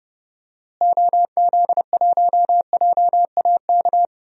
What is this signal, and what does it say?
Morse code, heard as a keyed tone, about 30 wpm, 700 Hz. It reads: OZ1JAK
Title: DXCC-100 Call Signs - 6 Characters - Repeated 2x - 30wpm